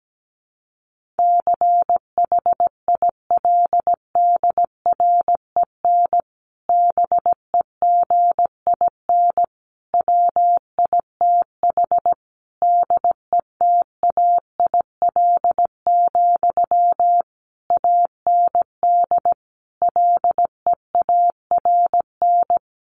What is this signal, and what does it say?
Morse code, heard as a keyed tone, about 17 wpm, 700 Hz. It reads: CHILDREN BEGIN WITH DETAIL, AND LEARN
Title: Sentences from Top 1500 Words 17wpm